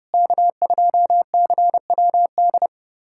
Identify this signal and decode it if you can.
Morse code, heard as a keyed tone, about 31 wpm, 700 Hz. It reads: K2CWD
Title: USA Callsign DE Callsign - 5 Characters - 30wpm